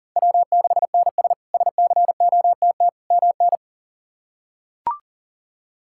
Morse code, heard as a keyed tone, about 40 wpm, 700 Hz. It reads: W6NS SCOTT MN
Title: K1USN SST QSO Practice - 40wpm